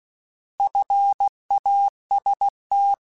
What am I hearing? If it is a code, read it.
Morse code, heard as a keyed tone, about 16 wpm, 800 Hz. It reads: FAST
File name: Alphabetical / Alphabetical mp3s / Fs @15